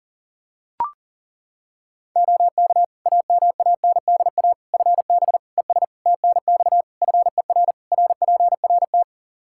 Morse code, heard as a keyed tone, about 40 wpm, 700 Hz. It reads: OK AMANDA FB ES TNX FER RPRT
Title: K1USN SST QSO Practice 40wpm